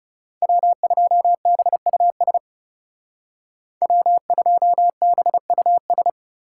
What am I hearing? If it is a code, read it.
Morse code, heard as a keyed tone, about 38 wpm, 700 Hz. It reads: W2BUS W2BUS